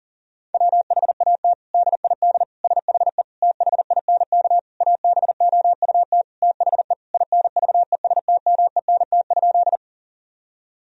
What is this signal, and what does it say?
Morse code, heard as a keyed tone, about 40 wpm, 700 Hz. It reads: WHAT DID SHE THINK ABOUT THE INVESTMENT?
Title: Sentences from Top 1000 Words - Review Part 4 - 40wpm